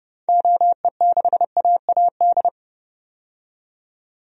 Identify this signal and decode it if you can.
Morse code, heard as a keyed tone, about 30 wpm, 700 Hz. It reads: OE6AAD
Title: DXCC-100 Call Signs - 6 Characters - 30wpm